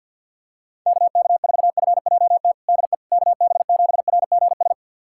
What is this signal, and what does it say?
Morse code, heard as a keyed tone, about 47 wpm, 700 Hz. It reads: KK4FJT DE KB7RGS
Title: USA Callsign DE Callsign - 6 Characters - 50wpm